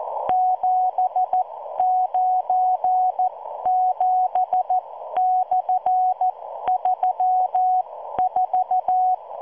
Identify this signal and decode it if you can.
Morse code, heard as a keyed tone, about 14 wpm, spaced as for 12 wpm, 740 Hz. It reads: 797/34